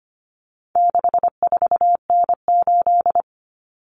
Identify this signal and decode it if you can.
Morse code, heard as a keyed tone, about 25 wpm, 700 Hz. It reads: 64N8
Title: Common Cut Numbers - 4-Digit 25wpm